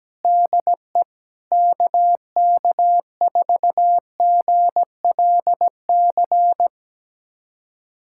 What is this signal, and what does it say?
Morse code, heard as a keyed tone, about 17 wpm, 700 Hz. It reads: DE KK4GLC
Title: USA Callsign DE Callsign 17wpm